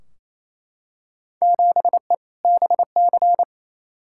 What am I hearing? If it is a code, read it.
Morse code, heard as a keyed tone, about 28 wpm, 700 Hz. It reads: 7E BC
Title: Field Day Exchanges Encore - 28wpm